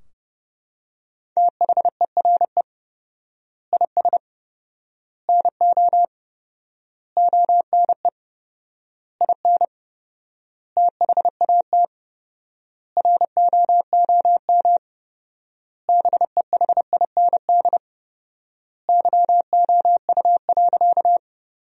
Morse code, heard as a keyed tone, about 30 wpm, 700 Hz. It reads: THERE IS NO ONE IN THAT ROOM BEHIND YOU.
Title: 4x Word-Spacing - Sentences from Top 500 Words - 30wpm